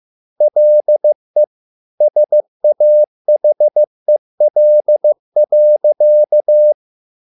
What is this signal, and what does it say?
Morse code, heard as a keyed tone, about 15 wpm, 600 Hz. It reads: LE SAHEL.